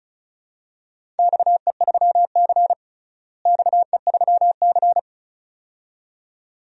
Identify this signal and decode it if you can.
Morse code, heard as a keyed tone, about 36 wpm, 700 Hz. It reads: XE3C XE3C
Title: DXCC-100 Call Signs - 4 Characters - Repeated 2x - 35wpm